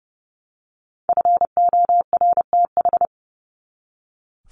Morse code, heard as a keyed tone, about 30 wpm, 700 Hz. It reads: FORTH
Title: Top 1751-2000 Words 30wpm